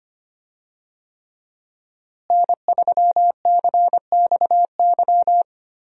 Morse code, heard as a keyed tone, about 25 wpm, 700 Hz. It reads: N3CXY
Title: USA Call Signs - 5 Characters - Repeated 3x - 25wpm